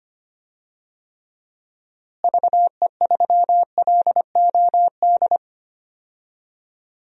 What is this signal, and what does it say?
Morse code, heard as a keyed tone, about 25 wpm, 700 Hz. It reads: VE3LOD
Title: DXCC-100 Call Signs - 6 Characters - Repeated 2x - 25wpm